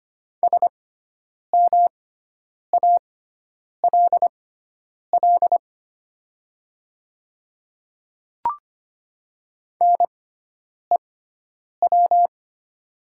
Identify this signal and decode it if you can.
Morse code, heard as a keyed tone, about 25 wpm, spaced as for 9 wpm, 690 Hz. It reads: SMALL NEW
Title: 6x Character Spacing - Top 300 Words - 25wpm